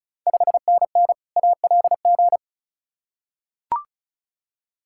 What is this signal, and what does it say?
Morse code, heard as a keyed tone, about 36 wpm, 690 Hz. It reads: 5NN ALG
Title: Ontario QSO Party Exchange 35wpm